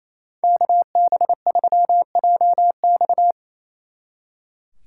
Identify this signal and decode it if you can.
Morse code, heard as a keyed tone, about 37 wpm, 700 Hz. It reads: KB3JX